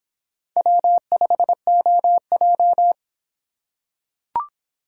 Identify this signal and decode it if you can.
Morse code, heard as a keyed tone, about 25 wpm, 700 Hz. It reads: W5OJ